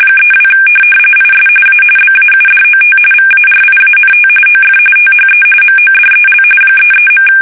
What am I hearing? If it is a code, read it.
RTTY, about 45.45 baud. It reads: 32 IS A BAUDOT VARIANT OF THE MODEL 33.
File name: fsk850 / teletype-corp